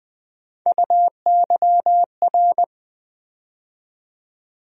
Morse code, heard as a keyed tone, about 20 wpm, 700 Hz. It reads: UYR
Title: USA Call Sign Postfixes - 3 Letters 20wpm